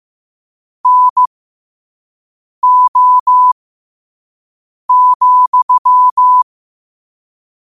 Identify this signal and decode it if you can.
Morse code, heard as a keyed tone, about 15 wpm, spaced as for 5 wpm, 1000 Hz. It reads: NO,
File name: 5wpm